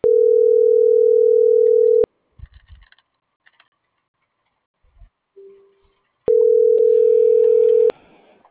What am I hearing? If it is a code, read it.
Ringback tone (North America), 440 + 480 Hz.